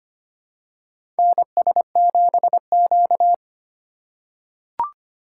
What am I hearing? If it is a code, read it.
Morse code, heard as a keyed tone, about 25 wpm, 700 Hz. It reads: NS7Q
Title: USA Call Signs - 2x Character Spacing - 25wpm